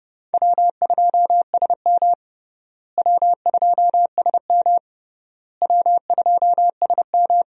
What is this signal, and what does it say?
Morse code, heard as a keyed tone, about 29 wpm, 700 Hz. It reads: W2SM W2SM W2SM
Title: USA Call Signs - 4 Characters - Repeated 3x - 30wpm